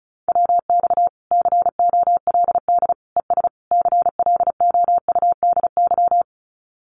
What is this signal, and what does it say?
Morse code, heard as a keyed tone, about 34 wpm, 700 Hz. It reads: WX COLD ES CLOUDY
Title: Common QSO Phrases Encore 35wpm